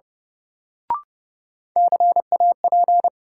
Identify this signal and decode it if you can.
Morse code, heard as a keyed tone, about 30 wpm, 700 Hz. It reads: CAP
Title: Sentences from Top 1600 Words - New Word Drill Alt - 30wpm